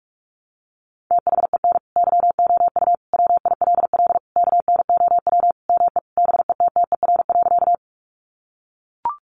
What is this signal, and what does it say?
Morse code, heard as a keyed tone, about 44 wpm, 700 Hz. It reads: THEN YOU WILL KNOW ME BETTER.
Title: Sentences from Top 400 Words 45wpm